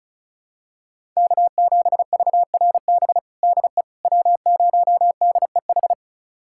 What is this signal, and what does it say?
Morse code, heard as a keyed tone, about 36 wpm, 700 Hz. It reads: K7VRB DE W0DEH
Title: USA Callsign DE Callsign - 5 Characters - 35wpm